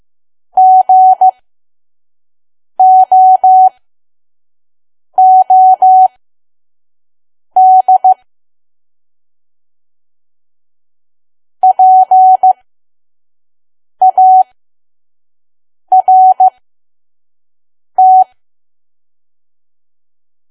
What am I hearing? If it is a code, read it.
Morse code, heard as a keyed tone, about 15 wpm, spaced as for 5 wpm, 750 Hz. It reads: GOOD PART
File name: Archive / 5 WPM files / PM